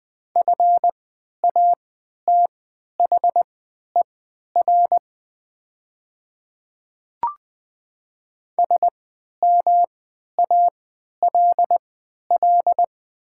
Morse code, heard as a keyed tone, about 20 wpm, spaced as for 11 wpm, 700 Hz. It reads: FATHER SMALL
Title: 3x Character Spacing - Top 500 Words - 20wpm